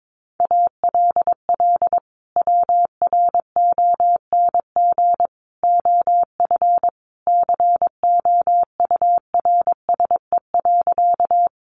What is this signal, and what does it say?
Morse code, heard as a keyed tone, about 22 wpm, 700 Hz. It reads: ALL WRONG OF COURSE.
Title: Sentences from Top 900 Words 22wpm